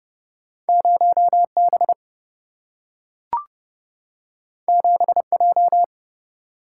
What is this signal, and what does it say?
Morse code, heard as a keyed tone, about 30 wpm, 700 Hz. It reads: 0B 7J
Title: Single-Letter Single-Number 30wpm